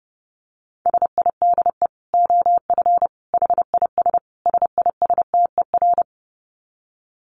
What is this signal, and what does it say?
Morse code, heard as a keyed tone, about 28 wpm, 700 Hz. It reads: SIDE OF HIS SISTER
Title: Dit Challenge 30wpm